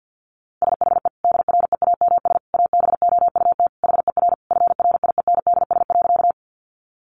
Morse code, heard as a keyed tone, about 50 wpm, 700 Hz. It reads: SHE DREAMS ABOUT HER FRIENDS.
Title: Easy Phrases 50wpm